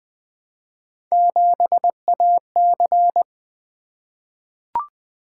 Morse code, heard as a keyed tone, about 20 wpm, 700 Hz. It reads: 7AC
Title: Three Letter-Number Combinations - 20wpm